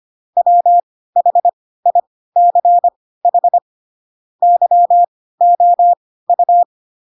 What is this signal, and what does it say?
Morse code, heard as a keyed tone, about 25 wpm, spaced as for 16 wpm, 700 Hz. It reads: WHICH YOU
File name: Pinocchio / Pinocchio ch2 10 25 ik1hge